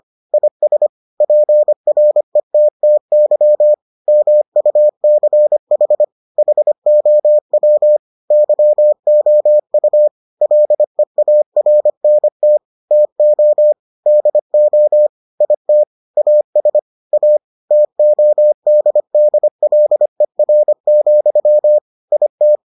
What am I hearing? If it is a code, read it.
Morse code, heard as a keyed tone, about 25 wpm, 600 Hz. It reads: IS PRETTY MUCH HOW YOU LEARNT TO DO IT AS A TODDLER, IT